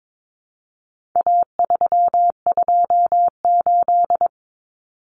Morse code, heard as a keyed tone, about 22 wpm, 700 Hz. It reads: A328
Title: Common Cut Numbers - 4-Digit 22wpm